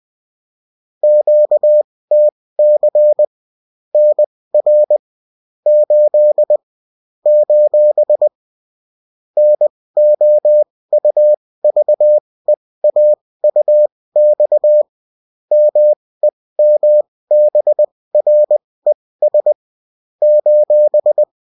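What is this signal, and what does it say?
Morse code, heard as a keyed tone, about 20 wpm, spaced as for 16 wpm, 600 Hz. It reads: QTC NR 8 : NOUVEAUX MEMBRES :
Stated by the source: Morse code bulletin of a French amateur CW union, sent as text and in CW.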